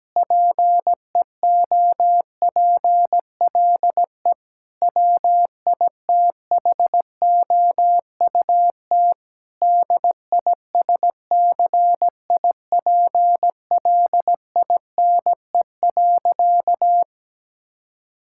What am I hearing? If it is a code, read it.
Morse code, heard as a keyed tone, about 17 wpm, 700 Hz. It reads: PEOPLE WITHOUT DISCIPLINE.
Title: Sentences from Top 1000 Words - Review Part 6 - 17wpm